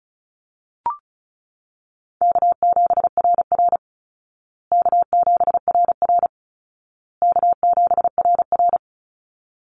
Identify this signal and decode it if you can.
Morse code, heard as a keyed tone, about 34 wpm, 700 Hz. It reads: K7RR K7RR K7RR
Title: Easy Call Signs Repeated 3x - 35wpm